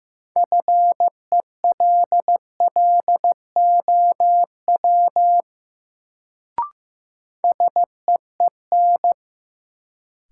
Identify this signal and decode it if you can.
Morse code, heard as a keyed tone, about 15 wpm, 700 Hz. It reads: FELLOW SEEN
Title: Sets of 5 Words Encore - New Word Drill - 15wpm